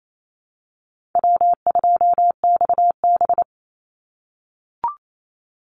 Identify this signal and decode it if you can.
Morse code, heard as a keyed tone, about 28 wpm, 700 Hz. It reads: W2XB
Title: Easy Call Signs - 28wpm